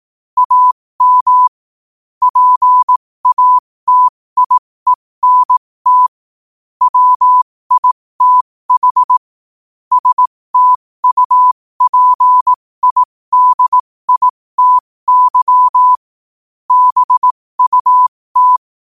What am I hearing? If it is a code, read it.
Morse code, heard as a keyed tone, about 18 wpm, spaced as for 15 wpm, 1000 Hz. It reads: AM PATIENT WITH STUPIDITY BUT
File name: qotdfiles / 15wpm